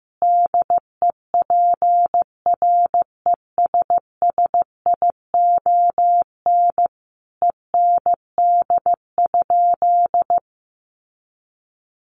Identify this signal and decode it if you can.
Morse code, heard as a keyed tone, about 15 wpm, 700 Hz. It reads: DEPRESSION END?
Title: Sentences from Top 1300 Words 15wpm